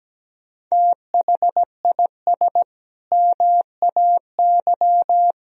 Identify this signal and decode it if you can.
Morse code, heard as a keyed tone, about 17 wpm, 700 Hz. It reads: THIS MAY
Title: Sentences from Top 1000 Words - Review Part 3 - 17wpm